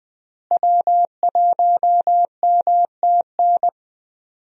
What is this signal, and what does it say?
Morse code, heard as a keyed tone, about 21 wpm, 700 Hz. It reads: W1MTN
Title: USA Call Signs - 5 Characters - Repeated 3x - 20wpm